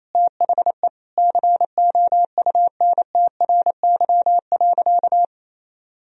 Morse code, heard as a keyed tone, about 28 wpm, 700 Hz. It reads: THE COUNTRY.
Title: Sentences from Top 300 Words and Abbreviations 28wpm